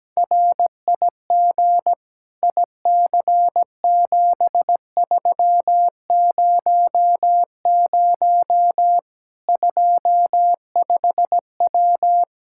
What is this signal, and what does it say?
Morse code, heard as a keyed tone, about 17 wpm, 700 Hz. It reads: RIG IC7300 25W
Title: K1USN SST QSO Practice 17wpm